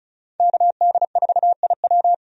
Morse code, heard as a keyed tone, about 36 wpm, 700 Hz. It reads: KD4IW
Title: USA Call Signs - 5 Characters - 35wpm